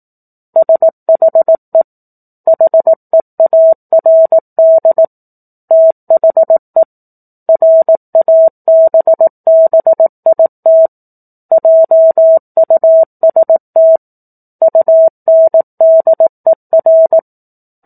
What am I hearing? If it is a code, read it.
Morse code, heard as a keyed tone, about 18 wpm, 650 Hz. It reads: SHE HEARD THE RABBIT JUST UNDER
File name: 04-Chapter-04 18 WPM